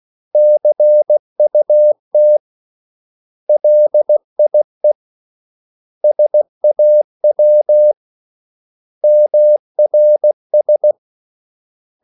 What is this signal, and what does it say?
Morse code, heard as a keyed tone, about 16 wpm, 600 Hz. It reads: CUT LIE SAW MRS
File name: CW / Practice / 500MostCommonEnglishWords16